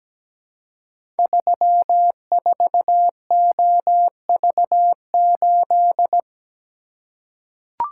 Morse code, heard as a keyed tone, about 17 wpm, 700 Hz. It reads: 34OV8
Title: Code Groups - 5 Letter:Numbers - 17wpm